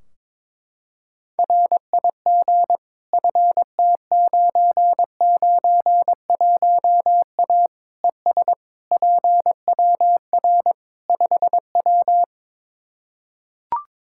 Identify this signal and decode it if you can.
Morse code, heard as a keyed tone, about 22 wpm, 700 Hz. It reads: RIG FT991A ES PWR 5W
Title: Practice QSO - Exchange 2 - Info - 22wpm